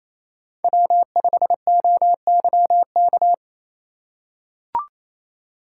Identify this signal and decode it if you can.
Morse code, heard as a keyed tone, about 28 wpm, 700 Hz. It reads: W5OYK
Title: USA Call Signs - 5 Characters - 2x Character Spacing - 28wpm